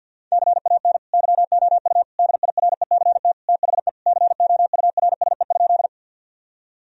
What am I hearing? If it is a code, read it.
Morse code, heard as a keyed tone, about 50 wpm, 700 Hz. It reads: CAN YOU DIRECT THE COURSE?
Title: Sentences from Top 1000 Words - Review Part 2 - 50wpm